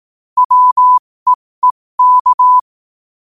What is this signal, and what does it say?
Morse code, heard as a keyed tone, about 18 wpm, spaced as for 15 wpm, 1000 Hz. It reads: WEEK
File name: qotdfiles / 15wpm